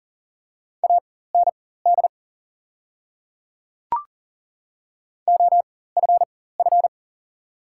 Morse code, heard as a keyed tone, about 39 wpm, spaced as for 18 wpm, 690 Hz. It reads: AND OFF
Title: 4x Character Spacing - Top 200 Words - 40wpm